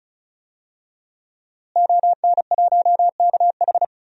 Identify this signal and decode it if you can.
Morse code, heard as a keyed tone, about 36 wpm, 700 Hz. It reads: ON1KH
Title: DXCC-100 Call Signs - 5 Characters - Repeated 3x - 35wpm